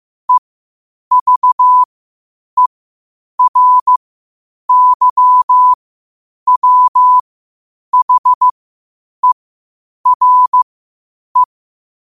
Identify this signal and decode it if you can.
Morse code, heard as a keyed tone, about 15 wpm, spaced as for 8 wpm, 1000 Hz. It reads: EVERYWHERE
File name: 8wpm